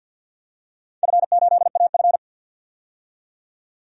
Morse code, heard as a keyed tone, about 49 wpm, 700 Hz. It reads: F8AF
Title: DXCC-100 Call Signs - 4 Characters - 50wpm